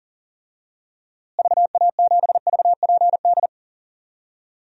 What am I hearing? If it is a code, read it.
Morse code, heard as a keyed tone, about 40 wpm, 700 Hz. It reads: VA7VPD
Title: DXCC-100 Call Signs - 6 Characters - Repeated 3x - 40wpm